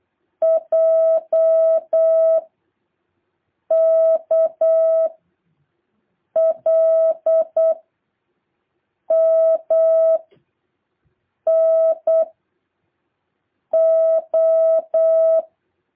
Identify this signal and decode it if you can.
Morse code, heard as a keyed tone, about 8 wpm, 650 Hz. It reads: J K L M N O
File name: jh8cbh / 2023cw